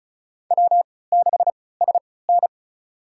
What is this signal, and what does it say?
Morse code, heard as a keyed tone, about 36 wpm, spaced as for 20 wpm, 700 Hz. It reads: W6SN
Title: USA Call Signs - 3x Character Spacing - 35wpm